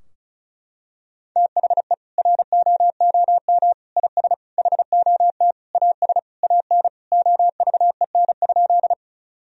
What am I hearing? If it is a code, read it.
Morse code, heard as a keyed tone, about 36 wpm, 700 Hz. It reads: THE ROOM IS HOT AS AN OVEN?
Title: Sentences from Top 1000 Words - Review Part 4 - 35wpm